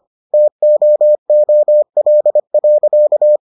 Morse code, heard as a keyed tone, about 25 wpm, 600 Hz. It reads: TOOL.